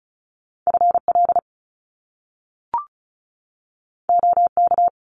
Morse code, heard as a keyed tone, about 35 wpm, 700 Hz. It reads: FL OK
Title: US States 35wpm